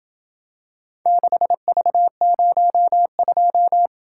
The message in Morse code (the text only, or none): 6V02